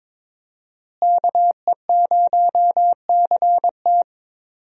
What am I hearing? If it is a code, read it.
Morse code, heard as a keyed tone, about 22 wpm, 700 Hz. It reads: KE0CT
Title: USA Call Signs - 5 Characters - Repeated 3x - 22wpm